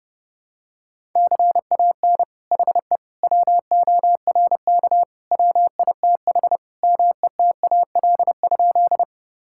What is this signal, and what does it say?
Morse code, heard as a keyed tone, about 29 wpm, 700 Hz. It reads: CAN HE WORK WITH METAL?
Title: Sentences from Top 1000 Words - Review Part 6 - 30wpm